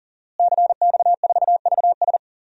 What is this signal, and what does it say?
Morse code, heard as a keyed tone, about 40 wpm, 700 Hz. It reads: CX4VS
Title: DXCC-100 Call Signs - 5 Characters - Repeated 3x - 40wpm